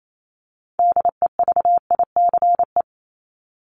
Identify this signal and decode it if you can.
Morse code, heard as a keyed tone, about 28 wpm, 700 Hz. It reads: DEVICE
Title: Top 1251-1500 Words 28wpm